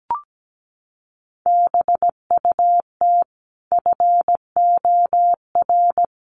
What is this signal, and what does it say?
Morse code, heard as a keyed tone, about 17 wpm, 700 Hz. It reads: BUT FOR
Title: Sentences from Top 1300 Words 17wpm